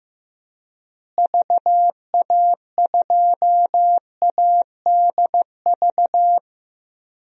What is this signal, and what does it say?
Morse code, heard as a keyed tone, about 15 wpm, 700 Hz. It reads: VA2ADV
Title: DXCC-100 Call Signs - 6 Characters - Repeated 3x - 15wpm